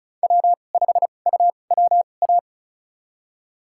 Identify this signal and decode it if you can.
Morse code, heard as a keyed tone, about 36 wpm, spaced as for 25 wpm, 700 Hz. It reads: W5UWA
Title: USA Call Signs - 5 Characters - 2x Character Spacing - 35wpm